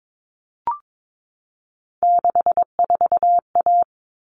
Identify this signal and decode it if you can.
Morse code, heard as a keyed tone, about 22 wpm, 700 Hz. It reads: 64A
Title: All Cut Numbers - 3-Digit 22wpm